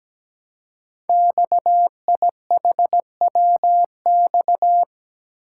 Morse code, heard as a keyed tone, about 17 wpm, 700 Hz. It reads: XIHWX
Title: Code Groups - 5 Letters - 17wpm